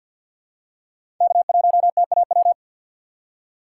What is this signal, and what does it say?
Morse code, heard as a keyed tone, about 50 wpm, 700 Hz. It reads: K1TAW
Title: USA Call Signs - 3x Character Spacing - 50wpm